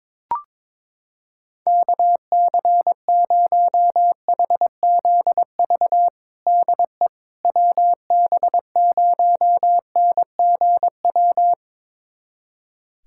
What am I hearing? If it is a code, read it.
Morse code, heard as a keyed tone, about 22 wpm, 700 Hz. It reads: KC0HZV DE WB0NGW
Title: USA Callsign DE Callsign - 6 Characters - 22wpm